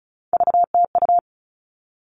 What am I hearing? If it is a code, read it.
Morse code, heard as a keyed tone, about 34 wpm, 700 Hz. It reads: VTU
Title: All Cut Numbers - 3-Digit 35wpm